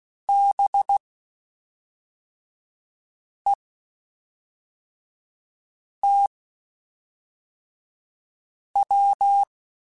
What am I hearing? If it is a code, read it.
Morse code, heard as a keyed tone, about 16 wpm, spaced as for 3 wpm, 800 Hz. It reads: BETW